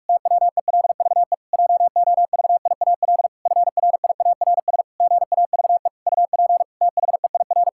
Morse code, heard as a keyed tone, about 46 wpm, 700 Hz. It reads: TWELVE JOVIAL FRIARS GAVE UP THEIR
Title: Pangram Challenge v2 - 45wpm